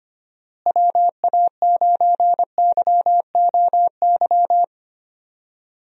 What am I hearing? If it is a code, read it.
Morse code, heard as a keyed tone, about 25 wpm, 700 Hz. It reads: WA9YOY
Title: USA Call Signs - 6 Characters - Repeated 2x - 25wpm